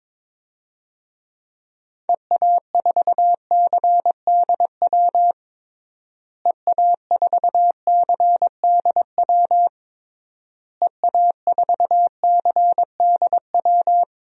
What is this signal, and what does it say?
Morse code, heard as a keyed tone, about 22 wpm, 700 Hz. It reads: EA4CDW EA4CDW EA4CDW
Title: DXCC-100 Call Signs - 6 Characters - Repeated 3x - 22wpm